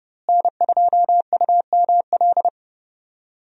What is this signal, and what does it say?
Morse code, heard as a keyed tone, about 30 wpm, 700 Hz. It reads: N2UML
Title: USA Call Signs - 5 Characters - Repeated 3x - 30wpm